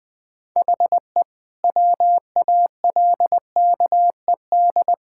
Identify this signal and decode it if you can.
Morse code, heard as a keyed tone, about 19 wpm, 700 Hz. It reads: HE WALKED
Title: Sentences with 5-Word Phrases - Spoken Beginning - 20wpm